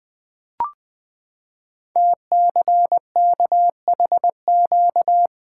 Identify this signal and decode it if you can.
Morse code, heard as a keyed tone, about 20 wpm, 700 Hz. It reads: TCKHQ
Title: Code Groups - 5 Letters - 20wpm